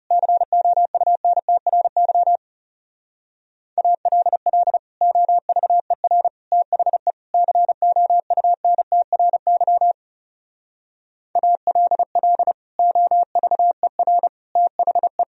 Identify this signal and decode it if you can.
Morse code, heard as a keyed tone, about 38 wpm, 700 Hz. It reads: COUNTRY ALL OVER THE COUNTRY ALL OVER THE
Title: Speed-Racing - Sets of 4 Words 30wpm